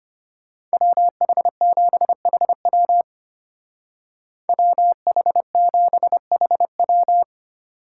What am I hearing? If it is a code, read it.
Morse code, heard as a keyed tone, about 30 wpm, 700 Hz. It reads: WH7HW WH7HW
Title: Speed Racing - USA 2x2 Call Signs 25wpm